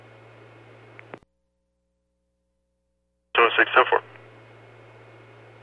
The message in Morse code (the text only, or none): none